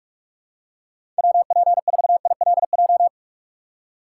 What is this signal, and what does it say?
Morse code, heard as a keyed tone, about 46 wpm, 700 Hz. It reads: WP4ILJ
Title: USA Call Signs - 6 Characters - 6x Character Spacing - 45wpm